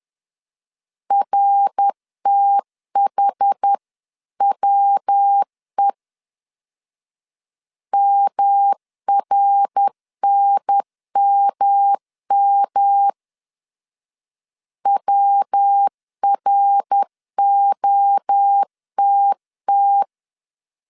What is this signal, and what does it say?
Morse code, heard as a keyed tone, about 10 wpm, 800 Hz. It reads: RTHWE MRNMM WROTT
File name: CW / DET / 1 3 1500C S C 10wpm 3335